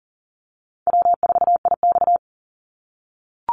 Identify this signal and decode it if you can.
Morse code, heard as a keyed tone, about 40 wpm, 700 Hz. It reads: W4IX
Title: Easy Call Signs - 40wpm